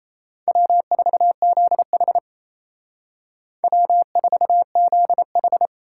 Morse code, heard as a keyed tone, about 34 wpm, 700 Hz. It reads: W4ZH W4ZH
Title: Speed Racing - USA 1x2 Call Signs 28wpm